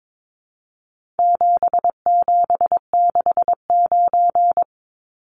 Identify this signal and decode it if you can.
Morse code, heard as a keyed tone, about 22 wpm, 700 Hz. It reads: 7769
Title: Numbers - 4-Digit 22wpm